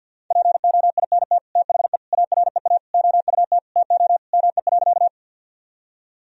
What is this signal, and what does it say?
Morse code, heard as a keyed tone, about 50 wpm, 700 Hz. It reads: POINT THE AREA OUT TO ME.
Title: Sentences from Top 1000 Words - Review Part 2 - 50wpm